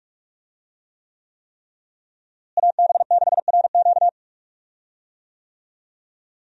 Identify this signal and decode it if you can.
Morse code, heard as a keyed tone, about 46 wpm, 700 Hz. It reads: AB6RQ
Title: USA Call Signs - 5 Characters - Repeated 3x - 45wpm